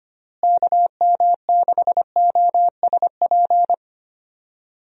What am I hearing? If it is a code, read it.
Morse code, heard as a keyed tone, about 25 wpm, 700 Hz. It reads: KM6OSP
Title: USA Call Signs - 6 Characters - Repeated 2x - 25wpm